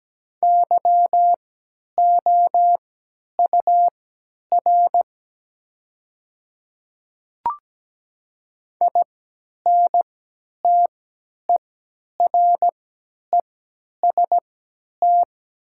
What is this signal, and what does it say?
Morse code, heard as a keyed tone, about 17 wpm, spaced as for 10 wpm, 700 Hz. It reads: YOUR INTEREST
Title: 3x Character Spacing - Top 500 Words - 17wpm